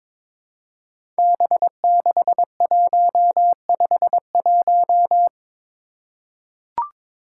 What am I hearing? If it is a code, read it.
Morse code, heard as a keyed tone, about 22 wpm, 700 Hz. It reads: B6151
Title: Five Challenging Letter-Number Combinations 22wpm